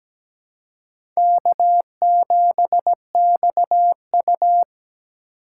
Morse code, heard as a keyed tone, about 17 wpm, 700 Hz. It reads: K7XU
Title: USA Call Signs - 4 Characters - 2x Character Spacing - 17wpm